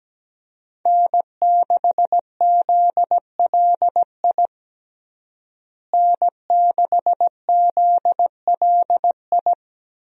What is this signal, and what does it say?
Morse code, heard as a keyed tone, about 17 wpm, 700 Hz. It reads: N6ZLI N6ZLI
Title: USA Call Signs - 5 Characters - Repeated 2x - 17wpm